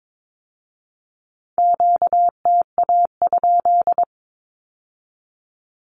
Morse code, heard as a keyed tone, about 22 wpm, 700 Hz. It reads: QTA?
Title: Q-Codes with Spoken Meaning - 22wpm